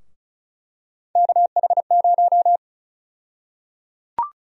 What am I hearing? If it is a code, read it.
Morse code, heard as a keyed tone, about 34 wpm, 700 Hz. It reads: KH0
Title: USA Call Sign Prefixes - 35wpm